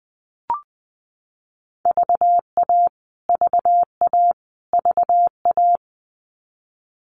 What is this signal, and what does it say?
Morse code, heard as a keyed tone, about 19 wpm, 700 Hz. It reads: VA VA VA
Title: US States Repeated 3x 20wpm